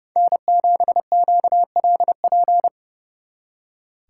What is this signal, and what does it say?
Morse code, heard as a keyed tone, about 37 wpm, 700 Hz. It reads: N7QLP
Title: Speed Racing - USA 1x3 Call Signs 30wpm